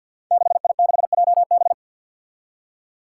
Morse code, heard as a keyed tone, about 47 wpm, 700 Hz. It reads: 6I6JB
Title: Five Challenging Letter-Number Combinations 50wpm